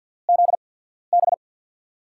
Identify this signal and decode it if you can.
Morse code, heard as a keyed tone, about 53 wpm, 700 Hz. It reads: Z B
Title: USA Call Signs - 6 Characters - 8x Character Spacing - 50wpm